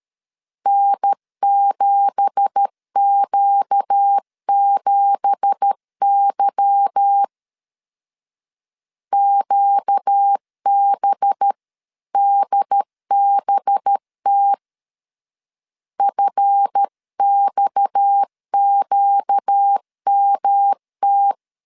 Morse code, heard as a keyed tone, about 13 wpm, 800 Hz. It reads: N7Q7Y QBDBT FXQMT
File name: CW / DET / 1 9 1500C S A 12wpm 3800